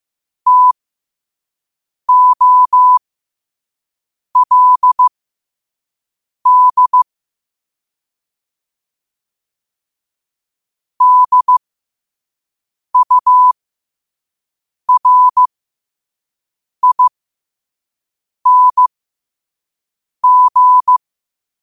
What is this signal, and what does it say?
Morse code, heard as a keyed tone, about 15 wpm, spaced as for 5 wpm, 1000 Hz. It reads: TOLD DURING
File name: qotdfiles / 5wpm